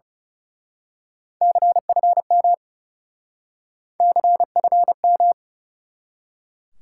Morse code, heard as a keyed tone, about 35 wpm, 700 Hz. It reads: CFM CFM
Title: Speed-Racing Top 90 QSO Elements 30wpm